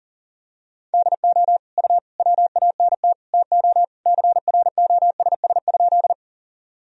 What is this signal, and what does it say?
Morse code, heard as a keyed tone, about 40 wpm, 700 Hz. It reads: DO U WANT TO CROSS?
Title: Sentences from Top 300 Words and Abbreviations 40wpm